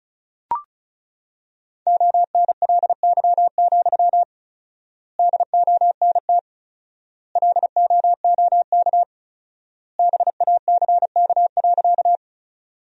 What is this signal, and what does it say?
Morse code, heard as a keyed tone, about 34 wpm, 700 Hz. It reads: ONLY, DONT LOOK BACK.
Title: 4x Word-Spacing - Sentences from Top 300 Words - 35wpm